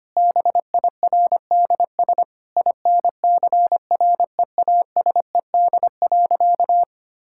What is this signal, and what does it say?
Morse code, heard as a keyed tone, about 25 wpm, 700 Hz. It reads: BIRDS INCREASED.
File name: practice / 25wpm / Sentences with 3-Word Phrases - 25wpm